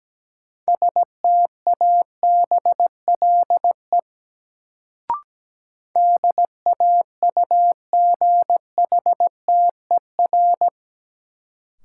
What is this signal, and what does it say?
Morse code, heard as a keyed tone, about 17 wpm, 700 Hz. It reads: STABLE DAUGHTER
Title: Sentences from Top 1500 Words - New Word Drill - 17wpm